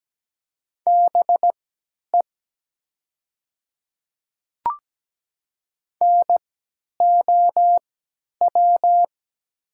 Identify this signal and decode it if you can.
Morse code, heard as a keyed tone, about 17 wpm, spaced as for 10 wpm, 700 Hz. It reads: BE NOW
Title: 3x Character Spacing - Top 100 Words - 17wpm